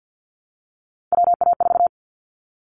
Morse code, heard as a keyed tone, about 50 wpm, 700 Hz. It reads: WA4
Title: Callsign Prefixes for Top 25 DXCC - Repeated 3x - 50wpm